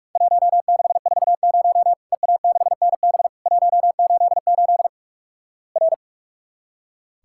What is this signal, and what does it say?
Morse code, heard as a keyed tone, about 44 wpm, 700 Hz. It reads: 1640 EA6NB 188 R
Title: Worked All Europe DX Contest - QTC Exchange - 45wpm